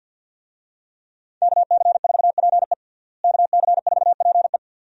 Morse code, heard as a keyed tone, about 50 wpm, 700 Hz. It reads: KC4PE KC4PE
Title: USA Call Signs - 5 Characters - Repeated 2x - 50wpm